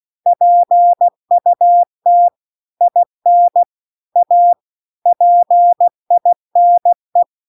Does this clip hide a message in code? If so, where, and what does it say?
Yes: Morse code from 0.3 to 7.2 s, heard as a keyed tone, about 16 wpm, 700 Hz. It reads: PUT IN A PINE